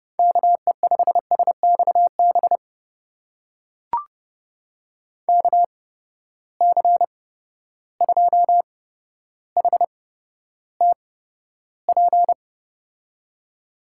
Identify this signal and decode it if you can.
Morse code, heard as a keyed tone, about 30 wpm, spaced as for 13 wpm, 700 Hz. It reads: KE5SXB KC2HTP
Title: USA Call Signs - 6 Characters - 8x Character Spacing - 30wpm